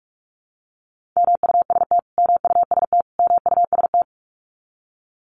Morse code, heard as a keyed tone, about 44 wpm, 700 Hz. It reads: MUST MUST MUST
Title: Top 500 Words Repeated 45wpm